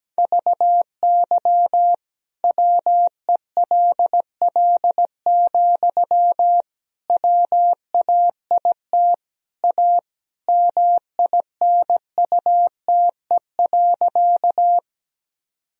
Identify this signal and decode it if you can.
Morse code, heard as a keyed tone, about 17 wpm, 700 Hz. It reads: VY WELL, WAIT A MINUTE.
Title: Sentences from Top 500 Words and Abbreviations 17wpm